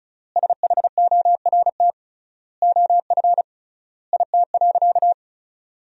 Morse code, heard as a keyed tone, about 36 wpm, 700 Hz. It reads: SHORT OF IT.
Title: 3x Word-Spacing - Sentences from Top 400 Words - 35wpm